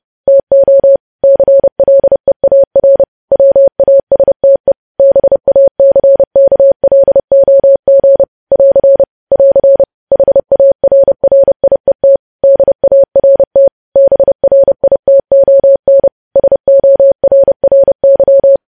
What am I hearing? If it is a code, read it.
Morse code, heard as a keyed tone, about 30 wpm, 550 Hz. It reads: TO CLEAR WASTE BACKLOG <AR> <AR> HARRIET DART BRITON SORRY
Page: news headlines in morse code at 30 wpm